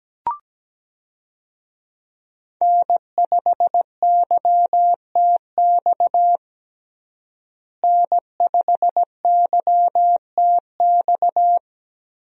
Morse code, heard as a keyed tone, about 17 wpm, 700 Hz. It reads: N5YTX N5YTX
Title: USA Call Signs - 5 Characters - Repeated 3x - 17wpm